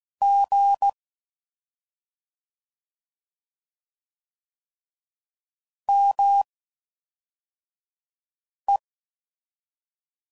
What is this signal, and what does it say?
Morse code, heard as a keyed tone, about 16 wpm, spaced as for 4 wpm, 800 Hz. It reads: G ME
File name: Mixed / CW Comparisons / CW Comparisons Broken Characters